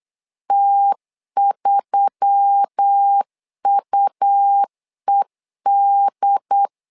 Morse code, heard as a keyed tone, about 8 wpm, 800 Hz. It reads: T3UED
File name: LEZIONI / 5 LBVC / 5 LB 8wpm 2816